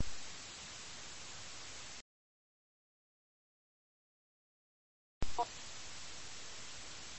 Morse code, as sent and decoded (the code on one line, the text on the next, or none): none